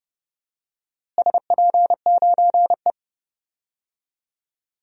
Morse code, heard as a keyed tone, about 30 wpm, 700 Hz. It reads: SP9E
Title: DXCC-100 Call Signs - 4 Characters - Repeated 3x - 30wpm